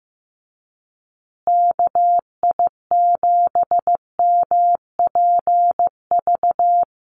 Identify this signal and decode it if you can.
Morse code, heard as a keyed tone, about 15 wpm, 700 Hz. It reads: KI7MPV
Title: USA Call Signs Repeated 3x 15wpm